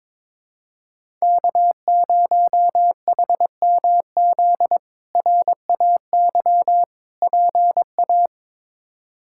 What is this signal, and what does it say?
Morse code, heard as a keyed tone, about 21 wpm, 700 Hz. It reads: K0HMZ RAY PA
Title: K1USN SST QSO Practice - 22wpm